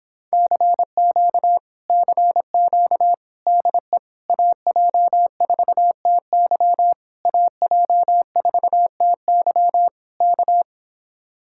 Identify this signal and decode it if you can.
Morse code, heard as a keyed tone, about 26 wpm, 700 Hz. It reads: CQ CQ DE AJ4TY AJ4TY K